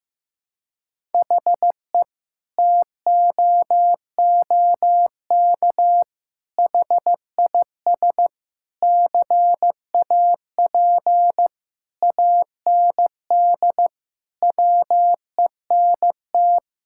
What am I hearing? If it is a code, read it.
Morse code, heard as a keyed tone, about 15 wpm, 700 Hz. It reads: HE TOOK HIS CAP AND WENT
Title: Sentences from Top 1600 Words 15wpm